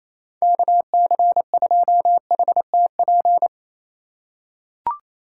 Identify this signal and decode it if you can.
Morse code, heard as a keyed tone, about 28 wpm, 700 Hz. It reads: KC2HTP
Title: USA Call Signs - 6 Characters - 6x Character Spacing - 28wpm